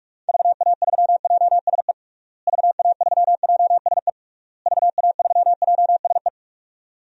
Morse code, heard as a keyed tone, about 46 wpm, 700 Hz. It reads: VA3JSE VA3JSE VA3JSE
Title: DXCC-100 Call Signs - 6 Characters - Repeated 3x - 45wpm